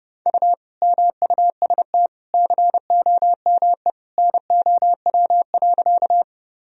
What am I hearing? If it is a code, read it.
Morse code, heard as a keyed tone, about 30 wpm, 700 Hz. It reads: U MUST COME NOW.
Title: Sentences from Top 200 Words and Abbreviations 30wpm